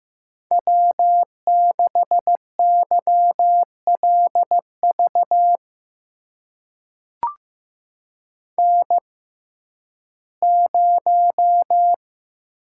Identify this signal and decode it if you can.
Morse code, heard as a keyed tone, about 15 wpm, spaced as for 11 wpm, 700 Hz. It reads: W6YLV N0
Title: USA Call Signs - 5 Characters - 6x Character Spacing - 15wpm